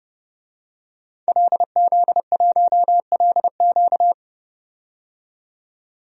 Morse code, heard as a keyed tone, about 29 wpm, 700 Hz. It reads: LZ1LQ
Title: DXCC-100 Call Signs - 5 Characters - 30wpm